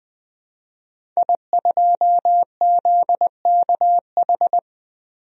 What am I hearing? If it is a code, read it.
Morse code, heard as a keyed tone, about 20 wpm, 700 Hz. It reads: I2ZKH
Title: DXCC-100 Call Signs - 5 Characters - 20wpm